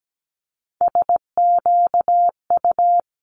Morse code, heard as a keyed tone, about 17 wpm, 700 Hz. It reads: SQU
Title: Multi-letter Phonemes Quick Fire 17wpm